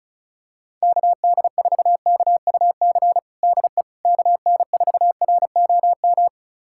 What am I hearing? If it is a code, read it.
Morse code, heard as a keyed tone, about 36 wpm, 700 Hz. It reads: KD4KUC DE KN4ROM
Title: USA Callsign DE Callsign 35wpm